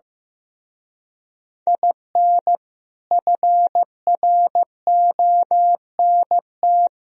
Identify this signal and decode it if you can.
Morse code, heard as a keyed tone, about 15 wpm, 700 Hz. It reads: IN FRONT
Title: Speed-Racing - Sets of 4 Words 15wpm